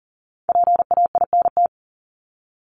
Morse code, heard as a keyed tone, about 40 wpm, 700 Hz. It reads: PAINT
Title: ING Suffix - Top 100 Words with Base Words 40wpm